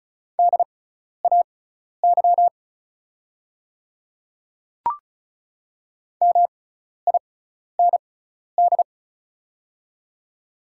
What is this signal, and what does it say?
Morse code, heard as a keyed tone, about 36 wpm, spaced as for 12 wpm, 700 Hz. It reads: DAY MIND
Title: 6x Character Spacing - Top 500 Words - 35wpm